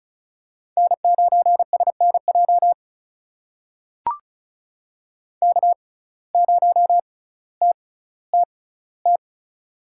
Morse code, heard as a keyed tone, about 36 wpm, spaced as for 18 wpm, 700 Hz. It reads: N9SNJ K0TTT
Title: USA Call Signs - 5 Characters - 6x Character Spacing - 35wpm